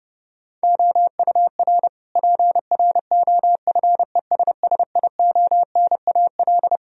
Morse code, heard as a keyed tone, about 32 wpm, 700 Hz. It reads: OUR PROFESSIONAL
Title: Sentences from Top 1000 Words - Review Part 4 - 30wpm